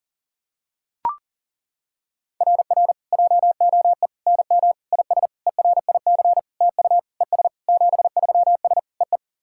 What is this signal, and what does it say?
Morse code, heard as a keyed tone, about 40 wpm, 700 Hz. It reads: RR JOE NM IS ERIC TU ES 73S EE
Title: SOTA QSO Practice 40wpm